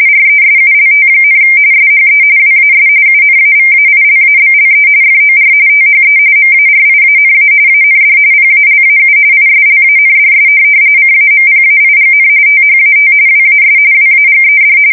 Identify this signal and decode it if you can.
RTTY, about 45.45 baud. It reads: RESULT, SHE WAS UNAVAILABLE FOR THE MARIANA AND PALAU ISLANDS CAMPAIGN THAT WAS WAGED